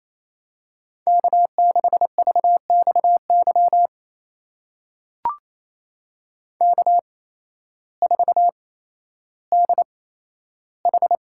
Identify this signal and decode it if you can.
Morse code, heard as a keyed tone, about 28 wpm, spaced as for 13 wpm, 700 Hz. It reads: K6VXY K4DH
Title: USA Call Signs - 8x Character Spacing - 28wpm